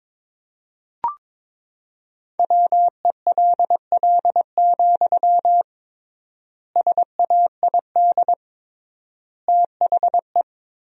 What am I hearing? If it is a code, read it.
Morse code, heard as a keyed tone, about 22 wpm, 700 Hz. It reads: WELL, SAID THE
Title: 3x Word-Spacing - Sentences from Top 400 Words - 22wpm